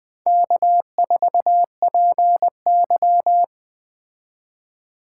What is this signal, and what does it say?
Morse code, heard as a keyed tone, about 20 wpm, 700 Hz. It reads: K4PY